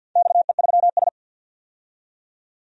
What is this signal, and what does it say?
Morse code, heard as a keyed tone, about 52 wpm, 700 Hz. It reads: XE3S
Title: DXCC-100 Call Signs - 4 Characters - 50wpm